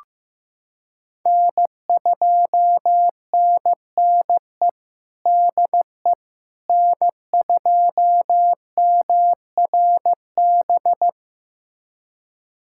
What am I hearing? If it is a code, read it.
Morse code, heard as a keyed tone, about 15 wpm, 700 Hz. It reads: N2NNE DE N2MRB
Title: USA Callsign DE Callsign - 5 Characters - 15wpm